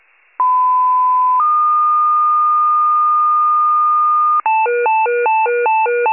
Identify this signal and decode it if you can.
Two-tone page, heard as a one-second tone, then a three-second tone, about 1010 then 1220 Hz. It is heard from 0.4 to 4.4 s.